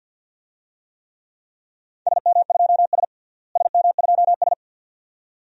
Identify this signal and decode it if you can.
Morse code, heard as a keyed tone, about 52 wpm, 700 Hz. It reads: SM2S SM2S
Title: DXCC-100 Call Signs - 4 Characters - Repeated 2x - 50wpm